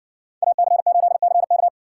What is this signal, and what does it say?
Morse code, heard as a keyed tone, about 60 wpm, 700 Hz. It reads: AC8CK